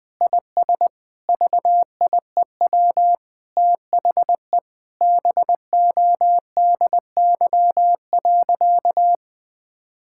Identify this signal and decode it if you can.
Morse code, heard as a keyed tone, about 20 wpm, 700 Hz. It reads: IS VIEW THE BODY.
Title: Sentences from Top 1000 Words - Review Part 6 - 20wpm